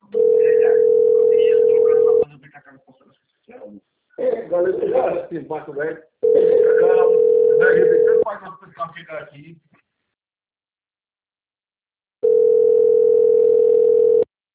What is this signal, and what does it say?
Ringback tone (North America), 440 + 480 Hz, with speech.
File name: rg-1600-unknown- - - .78349